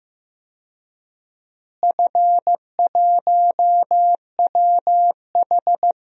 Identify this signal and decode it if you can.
Morse code, heard as a keyed tone, about 15 wpm, 700 Hz. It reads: F1WH
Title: DXCC-100 Call Signs - 4 Characters - Repeated 3x - 15wpm